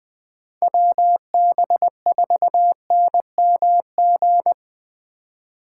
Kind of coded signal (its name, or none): Morse code